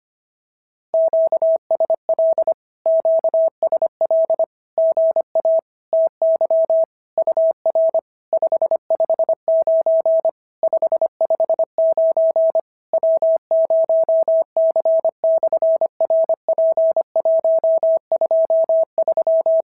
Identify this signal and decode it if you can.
Morse code, heard as a keyed tone, about 25 wpm, 690 Hz. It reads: QSL QSL GA TY UR 559 559 W0C/RP123
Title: SOTA QSO Practice 25wpm